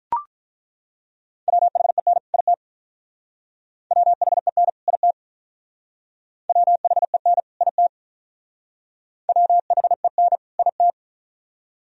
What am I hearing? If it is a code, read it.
Morse code, heard as a keyed tone, about 43 wpm, 700 Hz. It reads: WHEN IT WHEN IT WHEN IT WHEN IT
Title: Speed-Racing - Sets of 2 Words 35wpm